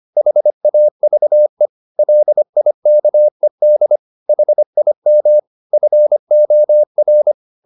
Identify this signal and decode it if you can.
Morse code, heard as a keyed tone, about 25 wpm, 600 Hz. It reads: HAVE LIKED HIM FOR
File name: the old man and the sea part 1 25wpm